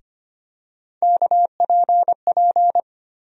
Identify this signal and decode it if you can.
Morse code, heard as a keyed tone, about 25 wpm, 700 Hz. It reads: KPP
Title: USA Call Sign Postfixes - 3 Letters 25wpm